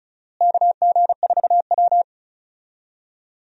Morse code, heard as a keyed tone, about 36 wpm, 700 Hz. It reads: KG4W
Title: USA Call Signs - 4 Characters - Repeated 3x - 35wpm